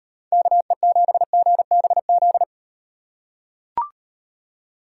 Morse code, heard as a keyed tone, about 35 wpm, 700 Hz. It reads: KE7GBZ